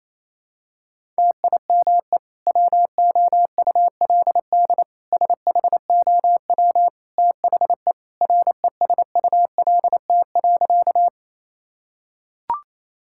Morse code, heard as a keyed tone, about 28 wpm, 700 Hz. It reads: TIME WOULD SHOW THE RESULT.
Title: Sentences from Top 1000 Words - Review Part 5 - 28wpm